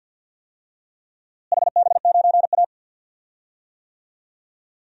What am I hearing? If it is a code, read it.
Morse code, heard as a keyed tone, about 49 wpm, 700 Hz. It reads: HB9A
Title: DXCC-100 Call Signs - 4 Characters - Repeated 2x - 50wpm